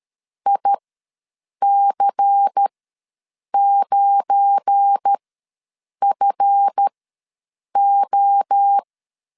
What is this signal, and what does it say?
Morse code, heard as a keyed tone, about 13 wpm, spaced as for 7 wpm, 800 Hz. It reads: IC9FO
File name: LEZIONI / 7 GF9Z / 7 GF9 ws 12wpm 3344